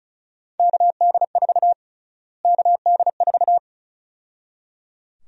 Morse code, heard as a keyed tone, about 36 wpm, 700 Hz. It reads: KD4 KD4
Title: USA Call Sign Prefixes - Repeated 2x - 35wpm